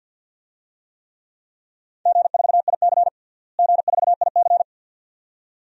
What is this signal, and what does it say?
Morse code, heard as a keyed tone, about 53 wpm, 700 Hz. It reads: G4IC G4IC
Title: DXCC-100 Call Signs - 4 Characters - Repeated 2x - 50wpm